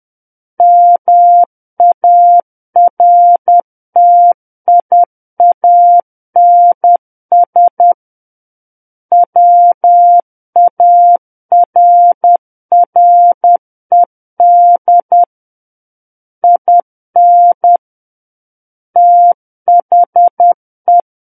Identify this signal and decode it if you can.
Morse code, heard as a keyed tone, about 10 wpm, 700 Hz. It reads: MARTIANS WARRED IN THE
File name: Book One / 01-Chapter 1 - 10 WPM